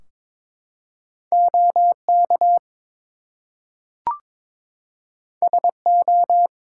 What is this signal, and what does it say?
Morse code, heard as a keyed tone, about 22 wpm, 700 Hz. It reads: OK SO
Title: Two Letter Words - 22wpm